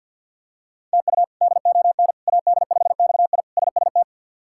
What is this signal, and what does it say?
Morse code, heard as a keyed tone, about 52 wpm, 700 Hz. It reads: TU DON AD5XI SST
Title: K1USN SST QSO Practice - 50wpm